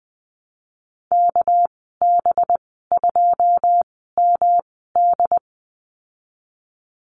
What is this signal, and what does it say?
Morse code, heard as a keyed tone, about 20 wpm, 700 Hz. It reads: K B 2 M D
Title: 231 Lesson 37g - Letter x - Call Signs 20wpm